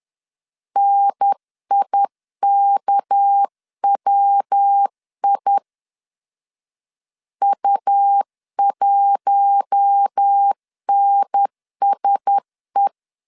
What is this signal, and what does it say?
Morse code, heard as a keyed tone, about 10 wpm, 800 Hz. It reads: NIKWI U1NSE
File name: CW / DET / 1 6 1500C S C 10wpm 4252